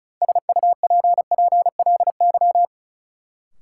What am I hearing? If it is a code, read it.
Morse code, heard as a keyed tone, about 36 wpm, 700 Hz. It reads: SUPPLY
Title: Sentences from Top 1200 Words - New Word Drill Alt - 35wpm